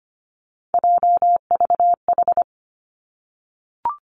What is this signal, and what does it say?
Morse code, heard as a keyed tone, about 25 wpm, 700 Hz. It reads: JVH
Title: Three Challenging Letter-Number Combinations 25wpm